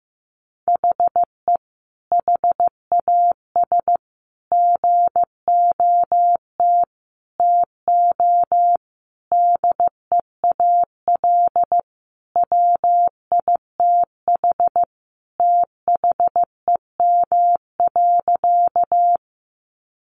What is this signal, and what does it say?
Morse code, heard as a keyed tone, about 14 wpm, 700 Hz. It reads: HE HAS GOT TO DEAL WITH THEM.
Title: Sentences from Top 800 Words 15wpm